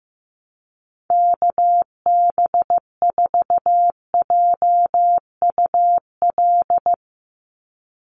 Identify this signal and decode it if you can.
Morse code, heard as a keyed tone, about 15 wpm, 700 Hz. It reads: KB4JUL
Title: USA Call Signs 15wpm